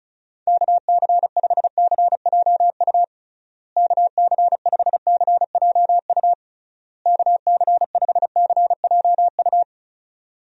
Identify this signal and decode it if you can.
Morse code, heard as a keyed tone, about 36 wpm, 700 Hz. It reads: KC5CJU KC5CJU KC5CJU
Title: USA Call Signs - 6 Characters - Repeated 3x - 35wpm